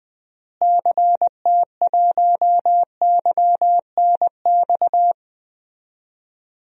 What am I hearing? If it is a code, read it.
Morse code, heard as a keyed tone, about 20 wpm, 700 Hz. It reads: CT1YNX
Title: DXCC-100 Call Signs - 6 Characters - Repeated 3x - 20wpm